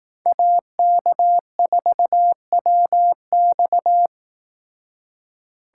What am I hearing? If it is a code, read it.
Morse code, heard as a keyed tone, about 18 wpm, 700 Hz. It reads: AK4WX
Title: Speed Racing - USA 2x2 Call Signs 15wpm